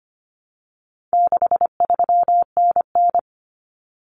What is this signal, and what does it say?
Morse code, heard as a keyed tone, about 25 wpm, 700 Hz. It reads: 63NN
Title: Common Cut Numbers - 4-Digit 25wpm